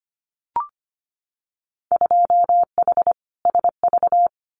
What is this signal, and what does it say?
Morse code, heard as a keyed tone, about 25 wpm, 700 Hz. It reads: 2H SV
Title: Winter Field Day Exchanges Encore 25wpm